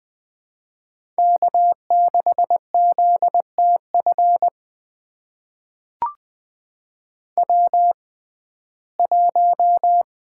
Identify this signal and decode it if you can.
Morse code, heard as a keyed tone, about 20 wpm, spaced as for 14 wpm, 700 Hz. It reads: K6ZTF W1
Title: USA Call Signs - 5 Characters - 6x Character Spacing - 20wpm